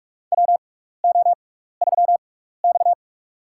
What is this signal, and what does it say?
Morse code, heard as a keyed tone, about 46 wpm, spaced as for 16 wpm, 700 Hz. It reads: WO3X
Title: USA Call Signs - 4 Characters - 6x Character Spacing - 45wpm